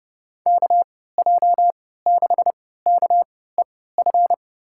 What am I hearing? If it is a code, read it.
Morse code, heard as a keyed tone, about 32 wpm, spaced as for 17 wpm, 700 Hz. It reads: KJ6KEF
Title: USA Call Signs - 6 Characters - 3x Character Spacing - 30wpm